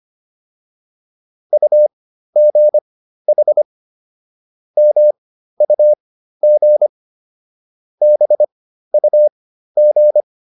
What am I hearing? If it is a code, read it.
Morse code, heard as a keyed tone, about 25 wpm, spaced as for 13 wpm, 600 Hz. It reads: UGH MUG BUG
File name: PR202 13